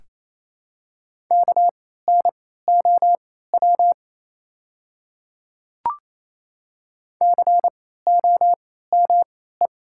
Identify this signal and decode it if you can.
Morse code, heard as a keyed tone, about 28 wpm, spaced as for 16 wpm, 700 Hz. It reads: KNOW COME
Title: 3x Character Spacing - Top 100 Words - 28wpm